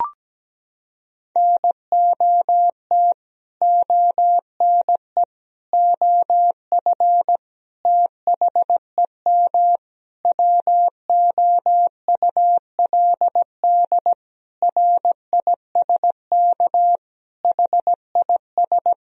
Morse code, heard as a keyed tone, about 17 wpm, 700 Hz. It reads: NOT ONE OF THEM WOULD RISK HIS
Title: Sentences from Top 1000 Words - Review Part 4 - 17wpm